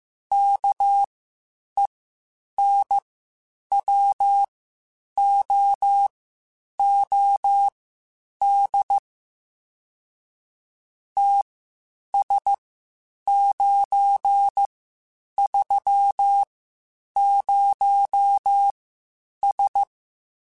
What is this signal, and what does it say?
Morse code, heard as a keyed tone, about 15 wpm, spaced as for 8 wpm, 800 Hz. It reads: KENWOOD TS930S
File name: QSOs Spaced-mp3s/kc7haa @13 sp